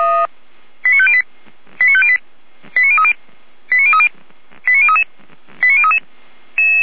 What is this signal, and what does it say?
Five-tone selective call (ZVEI), heard as quick runs of five tones, tones of about 70 ms. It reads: 89508 89508 89030 89030 89033 89033
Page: SDR/ZVEI